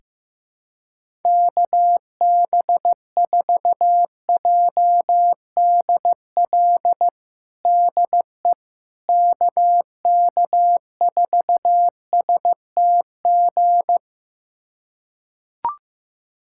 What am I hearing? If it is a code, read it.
Morse code, heard as a keyed tone, about 15 wpm, 700 Hz. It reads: KB4JDL DE KK4STG
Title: USA Callsign DE Callsign 15wpm